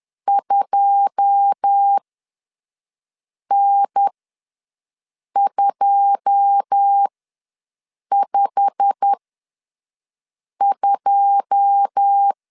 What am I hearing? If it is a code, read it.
Morse code, heard as a keyed tone, about 10 wpm, spaced as for 5 wpm, 800 Hz. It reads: 2N252
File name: LEZIONI / 9 2 ws 10wpm 4130